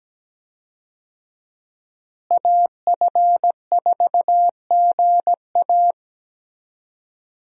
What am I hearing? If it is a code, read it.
Morse code, heard as a keyed tone, about 17 wpm, 700 Hz. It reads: AF4GA
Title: USA Call Signs - 5 Characters - Repeated 2x - 17wpm